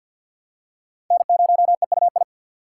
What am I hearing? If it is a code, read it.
Morse code, heard as a keyed tone, about 46 wpm, 700 Hz. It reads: N0EUI
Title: USA Call Signs - 5 Characters - 3x Character Spacing - 50wpm